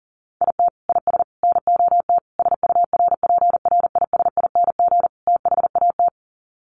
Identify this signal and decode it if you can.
Morse code, heard as a keyed tone, about 40 wpm, 700 Hz. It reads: IT IS NOT SURPRISING THAT
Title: Sets of 5 Words 40wpm